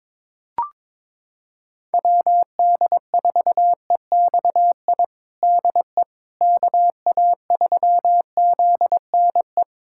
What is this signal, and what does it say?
Morse code, heard as a keyed tone, about 22 wpm, 700 Hz. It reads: WD4EXI DE KA3ZNE
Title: USA Callsign DE Callsign - 6 Characters - 22wpm